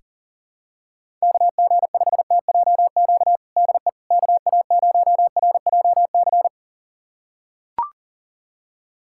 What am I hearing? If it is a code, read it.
Morse code, heard as a keyed tone, about 40 wpm, 700 Hz. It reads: KG5TJQ DE KA0RJC
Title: USA Callsign DE Callsign - 6 Characters - 40wpm